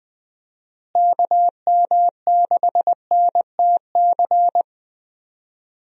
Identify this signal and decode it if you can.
Morse code, heard as a keyed tone, about 20 wpm, 700 Hz. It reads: KM6NTC
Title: USA Call Signs - 6 Characters - Repeated 3x - 20wpm